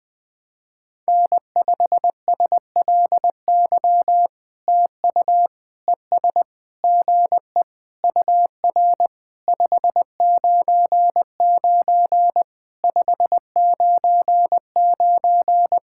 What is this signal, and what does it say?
Morse code, heard as a keyed tone, about 20 wpm, 700 Hz. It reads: N5SLY TU ES GE UR 599 599
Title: POTA QSO Practice - 20wpm